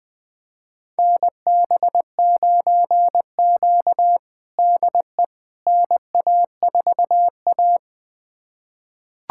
Morse code, heard as a keyed tone, about 20 wpm, 700 Hz. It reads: NB9Q DE NA4A
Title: USA Callsign DE Callsign - 4 Characters - 20wpm